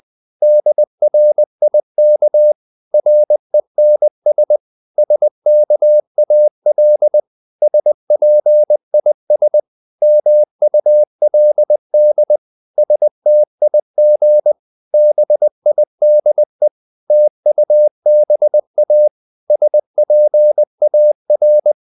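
Morse code, heard as a keyed tone, about 20 wpm, 600 Hz. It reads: DRIK RENS SKAL SPIS MULD STIG BIDE TUBA SPAR
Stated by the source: Morse code practice of short Danish words at 20 wpm, from CW med Gnister.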